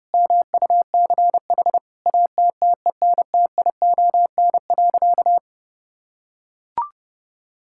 Morse code, heard as a keyed tone, about 30 wpm, 700 Hz. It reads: MUCH ATTENTION.
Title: Sentences from Top 1000 Words - Review Part 6 - 30wpm